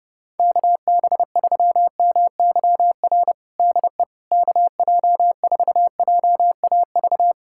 Morse code, heard as a keyed tone, about 30 wpm, 700 Hz. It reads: KB3MYR DE KJ4JAV
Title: USA Callsign DE Callsign - 6 Characters - 30wpm